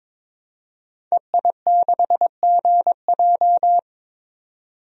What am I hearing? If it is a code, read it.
Morse code, heard as a keyed tone, about 22 wpm, 700 Hz. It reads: EI6GJ
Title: DXCC-100 Call Signs - 5 Characters - 22wpm